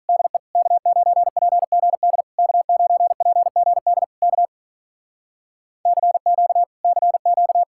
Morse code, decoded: DE K9PGD K9PGD K CQ CQ

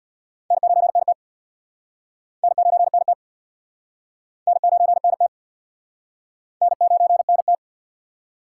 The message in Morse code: N9NT N9NT N9NT N9NT

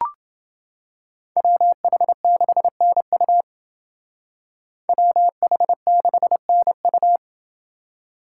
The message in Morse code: WH6NU WH6NU

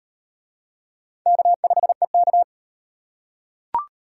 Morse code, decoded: K5EK

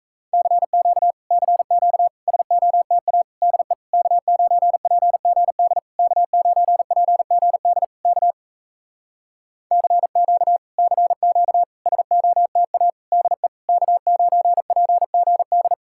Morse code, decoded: CQ CQ SOTA DE K9PGD K9PGD K CQ CQ SOTA DE K9PGD

